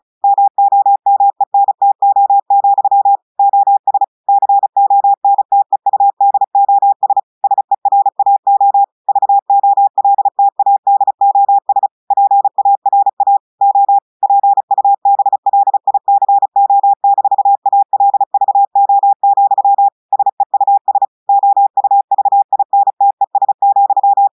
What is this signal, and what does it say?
Morse code, heard as a keyed tone, about 36 wpm, 800 Hz. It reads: MOMENTO, OS CONTEUDOS SERAO VOLTADOS PARA O PUBLICO-ALVO, SEUS OUVINTES,
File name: audios / portugais / portugais texte en clair 2 vitesse 35 mots minutes